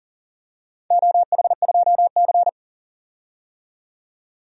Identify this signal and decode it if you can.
Morse code, heard as a keyed tone, about 37 wpm, 700 Hz. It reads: OH2C
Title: DXCC-100 Call Signs - 4 Characters - 40wpm